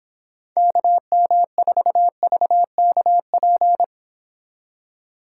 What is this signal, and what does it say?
Morse code, heard as a keyed tone, about 26 wpm, 700 Hz. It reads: KM4VKP